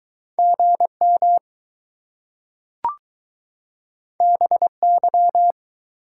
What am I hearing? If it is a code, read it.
Morse code, heard as a keyed tone, about 21 wpm, 700 Hz. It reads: GM BY